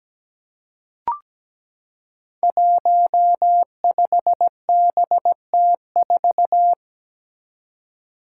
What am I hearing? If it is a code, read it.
Morse code, heard as a keyed tone, about 17 wpm, 700 Hz. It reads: 15BT4
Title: Five Challenging Letter-Number Combinations 17wpm